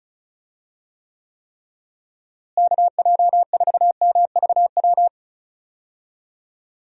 Morse code, heard as a keyed tone, about 34 wpm, 700 Hz. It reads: KJ4MVW
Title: USA Call Signs - 6 Characters - Repeated 3x - 35wpm